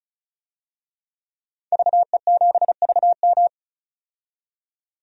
Morse code, heard as a keyed tone, about 36 wpm, 700 Hz. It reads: VE7VM
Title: DXCC-100 Call Signs - 5 Characters - 35wpm